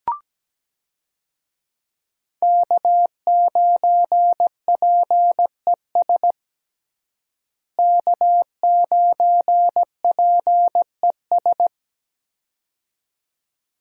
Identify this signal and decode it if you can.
Morse code, heard as a keyed tone, about 17 wpm, 700 Hz. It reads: K9PES K9PES
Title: USA Call Signs - 5 Characters - Repeated 2x - 17wpm